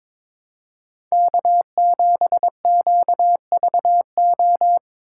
Morse code, heard as a keyed tone, about 25 wpm, 700 Hz. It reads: K7QVO
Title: Speed Racing - USA 1x3 Call Signs 22wpm